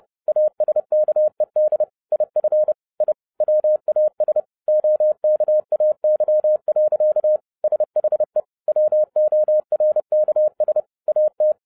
Morse code, heard as a keyed tone, about 30 wpm, 610 Hz. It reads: ASKED IF I WAS OKAY. SHE WORKS AT